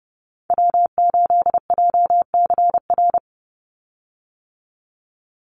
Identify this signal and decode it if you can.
Morse code, heard as a keyed tone, about 30 wpm, 700 Hz. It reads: W8JCR
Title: USA Call Signs 30wpm